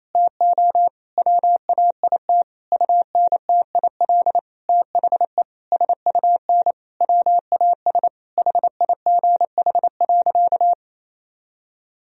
Morse code, heard as a keyed tone, about 28 wpm, 700 Hz. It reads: TO WAIT UNTIL THE SUN WAS HIGH.
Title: Sentences from Top 1000 Words - Review Part 2 - 28wpm